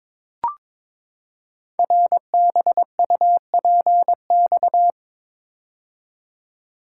Morse code, heard as a keyed tone, about 22 wpm, 700 Hz. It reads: RBUPX
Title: Code Groups - 5 Letters - 22wpm